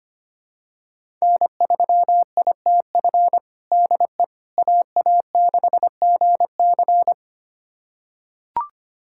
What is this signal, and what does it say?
Morse code, heard as a keyed tone, about 25 wpm, 700 Hz. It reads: N3ITF DE AA6GC
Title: USA Callsign DE Callsign - 5 Characters - 25wpm